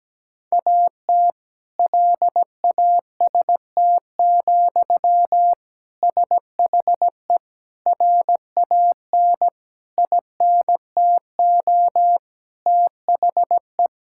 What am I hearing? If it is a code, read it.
Morse code, heard as a keyed tone, about 17 wpm, 700 Hz. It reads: AT LAST, SHE RAN INTO THE
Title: Sentences from Top 1600 Words 17wpm